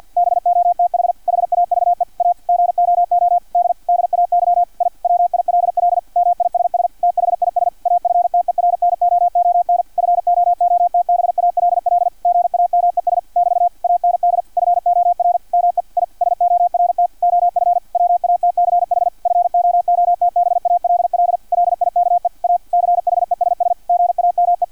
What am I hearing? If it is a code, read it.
Morse code, heard as a keyed tone, about 50 wpm, 700 Hz, with radio noise. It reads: BOTH HAVE A GOOD DAY I WILL MISS THIS AFTERNOON FOOTBALL GAMES <BT> AND FOR ME I SORT OF WATCH FOOTBALL LIKE A CHESS GAME
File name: archived / CRSnet